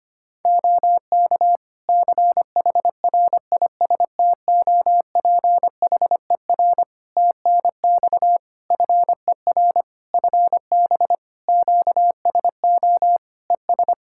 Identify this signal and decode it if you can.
Morse code, heard as a keyed tone, about 25 wpm, 700 Hz. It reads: OK CHRISTOPHER TNX FER FB QSO ES
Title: Practice QSO - Exchange 4 - Ending - 25wpm